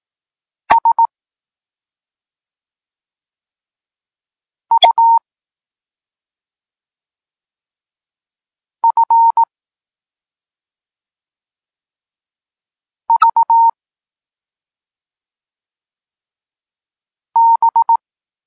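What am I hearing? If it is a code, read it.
Morse code, heard as a keyed tone, about 18 wpm, 920 Hz. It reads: S U F V B
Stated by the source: Polish Morse lesson teaching the letter C, from SP9MOA Niepołomicki Klub Krótkofalowców.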